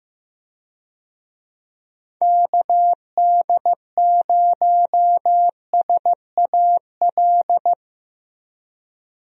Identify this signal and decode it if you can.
Morse code, heard as a keyed tone, about 15 wpm, 700 Hz. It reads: KD0SAL
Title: USA Call Signs - 6 Characters - 15wpm